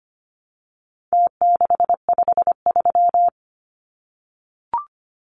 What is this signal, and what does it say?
Morse code, heard as a keyed tone, about 25 wpm, 700 Hz. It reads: T653
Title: Numbers - Contest Style - 4-Digit 25wpm